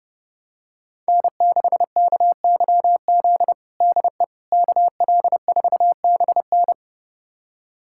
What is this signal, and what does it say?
Morse code, heard as a keyed tone, about 30 wpm, 700 Hz. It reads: N6KYZ DE KL4BN
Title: USA Callsign DE Callsign - 5 Characters - 30wpm